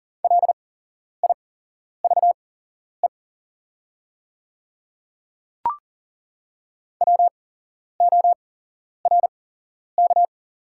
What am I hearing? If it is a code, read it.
Morse code, heard as a keyed tone, about 40 wpm, spaced as for 11 wpm, 690 Hz. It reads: LIVE WORK
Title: 8x Character Spacing - Top 500 Words - 40wpm